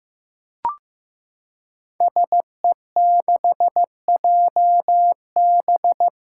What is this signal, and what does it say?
Morse code, heard as a keyed tone, about 15 wpm, 700 Hz. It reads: SE6JB
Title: Five Challenging Letter-Number Combinations 15wpm